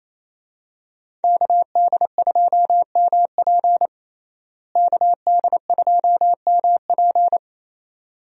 Morse code, heard as a keyed tone, about 27 wpm, 700 Hz. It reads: KD2MP KD2MP
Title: USA Call Signs - 5 Characters - Repeated 2x - 28wpm